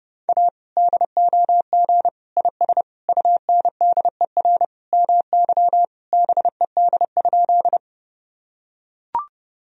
Morse code, heard as a keyed tone, about 30 wpm, 700 Hz. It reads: A DOG IS UNDER MY BED?
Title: Sentences from Top 1000 Words - Review Part 2 - 30wpm